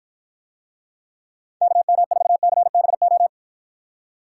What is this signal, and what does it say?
Morse code, heard as a keyed tone, about 53 wpm, 700 Hz. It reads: KM4CBO